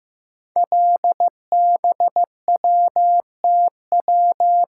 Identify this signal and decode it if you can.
Morse code, heard as a keyed tone, about 15 wpm, 700 Hz. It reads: LBWTW
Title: Code Groups - 5 Letter:Numbers - 15wpm